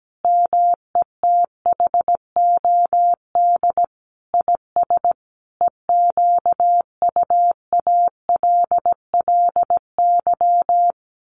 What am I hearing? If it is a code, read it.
Morse code, heard as a keyed tone, about 17 wpm, 700 Hz. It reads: METHOD IS EQUALLY
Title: Sentences from Top 1400 Words 17wpm